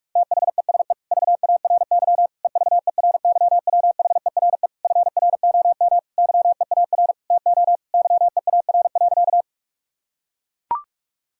Morse code, heard as a keyed tone, about 41 wpm, 700 Hz. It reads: THESE VARY EVERYWHERE FROM YEAR TO YEAR.
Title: Sentences from Top 1600 Words 45wpm